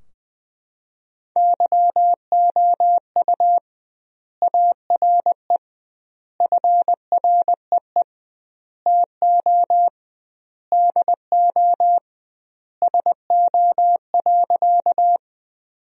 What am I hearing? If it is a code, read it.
Morse code, heard as a keyed tone, about 20 wpm, 700 Hz. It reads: YOU ARE FREE TO DO SO.
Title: 2x Word-Spacing - Sentences from Top 500 Words - 20wpm